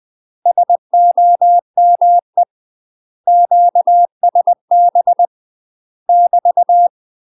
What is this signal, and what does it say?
Morse code, heard as a keyed tone, about 20 wpm, 700 Hz. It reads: SOME QSB <BT>
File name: UK MASTER QSOs TRAINING / 20WPM / UK MASTER QSOs 0046